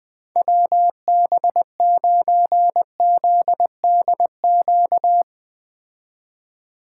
Morse code, heard as a keyed tone, about 20 wpm, 700 Hz. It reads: WB9ZDQ